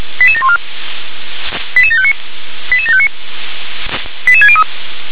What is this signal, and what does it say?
Five-tone selective call (ZVEI), heard as quick runs of five tones, tones of about 70 ms. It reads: 88714 88759 88759 88603
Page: SDR/ZVEI